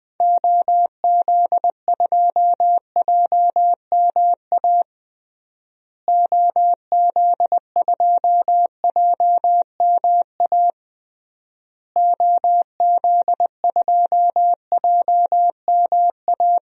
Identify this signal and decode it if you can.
Morse code, heard as a keyed tone, about 20 wpm, 700 Hz. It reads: OZ2JMA OZ2JMA OZ2JMA
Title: DXCC-100 Call Signs - 6 Characters - Repeated 3x - 20wpm